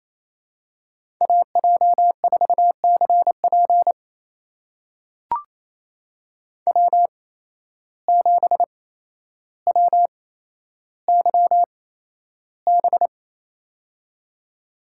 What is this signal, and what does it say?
Morse code, heard as a keyed tone, about 29 wpm, spaced as for 12 wpm, 700 Hz. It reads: AJ4CP W7WYB
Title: USA Call Signs - 5 Characters - 8x Character Spacing - 28wpm